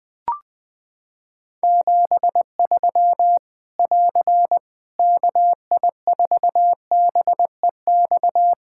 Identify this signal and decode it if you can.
Morse code, heard as a keyed tone, about 20 wpm, 700 Hz. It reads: 73 <AR> KI4BEX
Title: Practice QSO - Exchange 4 - Ending - 20wpm